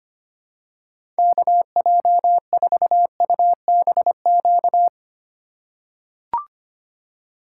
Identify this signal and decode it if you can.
Morse code, heard as a keyed tone, about 25 wpm, 700 Hz. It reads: KJ4UBQ
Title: USA Call Signs - 2x Character Spacing - 25wpm